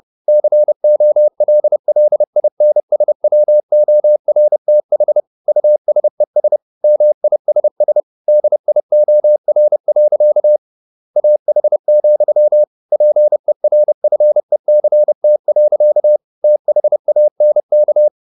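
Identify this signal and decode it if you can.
Morse code, heard as a keyed tone, about 30 wpm, 600 Hz. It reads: COLLINSWORTH USES MISS DIOR. AH, PERFECT. THANK
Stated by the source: Morse code at 30 wpm, the novel Never by Ken Follett in English, from CW med Gnister.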